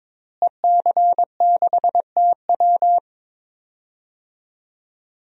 Morse code, heard as a keyed tone, about 22 wpm, 700 Hz. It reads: EC6TW
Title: DXCC-100 Call Signs - 5 Characters - Repeated 3x - 22wpm